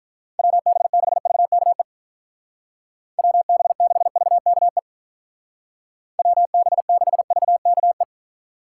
Morse code, heard as a keyed tone, about 43 wpm, 700 Hz. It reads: WB6VKE WB6VKE WB6VKE
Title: Speed Racing - USA 2x3 Call Signs 35wpm